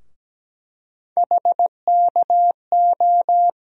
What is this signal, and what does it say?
Morse code, heard as a keyed tone, about 17 wpm, 700 Hz. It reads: HKO
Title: USA Call Sign Postfixes - 3 Letters 17wpm